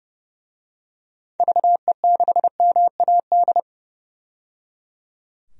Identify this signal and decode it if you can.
Morse code, heard as a keyed tone, about 30 wpm, 700 Hz. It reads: VE6MAD
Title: DXCC-100 Call Signs - 6 Characters - 30wpm